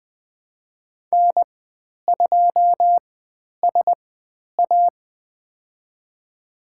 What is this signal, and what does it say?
Morse code, heard as a keyed tone, about 20 wpm, spaced as for 10 wpm, 700 Hz. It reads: N2SA
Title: 095 Lesson 16g - Letter c - Call Signs 20wpm